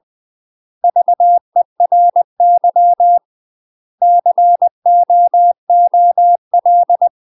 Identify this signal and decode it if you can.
Morse code, heard as a keyed tone, about 20 wpm, 700 Hz. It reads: VERY COOL
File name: UK MASTER QSOs TRAINING / 20WPM / UK MASTER QSOs 0032